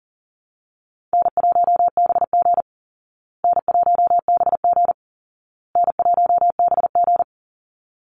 Morse code, heard as a keyed tone, about 40 wpm, 700 Hz. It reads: N1BG N1BG N1BG
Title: Easy Call Signs Repeated 3x - 40wpm